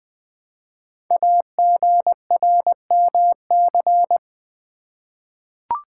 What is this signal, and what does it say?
Morse code, heard as a keyed tone, about 20 wpm, 700 Hz. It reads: AGRMC
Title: Code Groups - 5 Letters - 20wpm